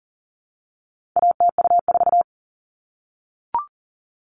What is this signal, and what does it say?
Morse code, heard as a keyed tone, about 40 wpm, 700 Hz. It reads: ATU4
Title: All Cut Numbers - 4-Digit 40wpm